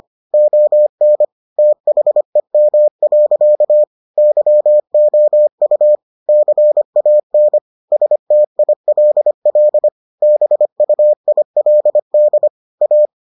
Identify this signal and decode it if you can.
Morse code, heard as a keyed tone, about 25 wpm, 600 Hz. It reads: ON THEM. YOU CAN STILL BUILD A